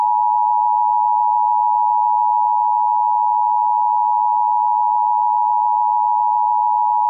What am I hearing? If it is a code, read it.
Emergency Alert System (SAME) tones (the 853 and 960 Hz attention tone), with noise. It is heard from 1.1 to 6.3 s.